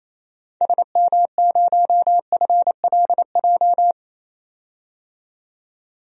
Morse code, heard as a keyed tone, about 28 wpm, 700 Hz. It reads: SM0FLJ
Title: DXCC-100 Call Signs - 6 Characters - 28wpm